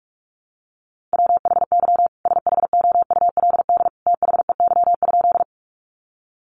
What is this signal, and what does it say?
Morse code, heard as a keyed tone, about 44 wpm, 700 Hz. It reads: WHY SHOULD THEY?
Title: Sentences from Top 300 Words 45wpm